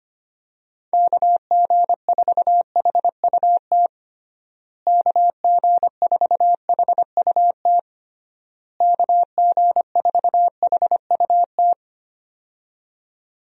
Morse code, heard as a keyed tone, about 25 wpm, 700 Hz. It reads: KG4HUT KG4HUT KG4HUT
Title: USA Call Signs - 6 Characters - Repeated 3x - 25wpm